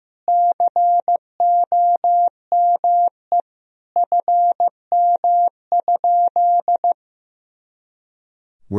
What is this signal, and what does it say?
Morse code, heard as a keyed tone, about 15 wpm, 700 Hz. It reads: COME FM?
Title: Sentences from Top 200 Words and Abbreviations 15wpm